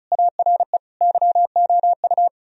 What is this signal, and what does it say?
Morse code, heard as a keyed tone, about 46 wpm, spaced as for 41 wpm, 700 Hz. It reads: ARE YOU
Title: Speed-Racing - Sets of 5 Words 35wpm